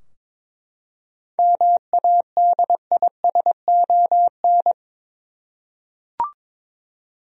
Morse code, heard as a keyed tone, about 22 wpm, 700 Hz. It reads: MADISON
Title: USA Common Names - 22wpm